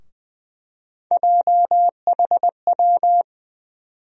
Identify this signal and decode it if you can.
Morse code, heard as a keyed tone, about 20 wpm, 700 Hz. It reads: JHW
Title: USA Call Sign Postfixes - 3 Letters 20wpm